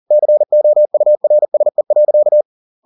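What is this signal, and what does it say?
Morse code, heard as a keyed tone, about 40 wpm, 600 Hz. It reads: COURSE.
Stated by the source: Morse code at 40 wpm, practice text, CW med Gnister.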